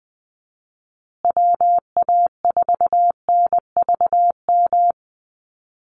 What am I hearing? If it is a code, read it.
Morse code, heard as a keyed tone, about 20 wpm, 700 Hz. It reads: WA4NVM
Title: USA Call Signs Repeated 2x 20wpm